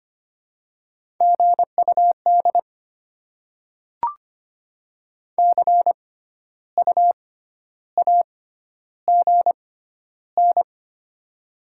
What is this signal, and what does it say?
Morse code, heard as a keyed tone, about 25 wpm, spaced as for 12 wpm, 700 Hz. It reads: GUD CU AGN
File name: Top 20 QSO Elements - 6x Character Spacing - 25wpm